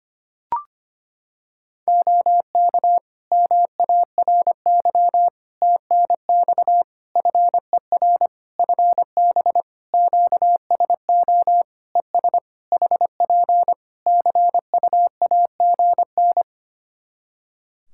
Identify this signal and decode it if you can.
Morse code, heard as a keyed tone, about 25 wpm, 700 Hz. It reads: OK MARY TNX FER FB QSO ES HP CUAGN
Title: Practice QSO - Exchange 4 - Ending - 25wpm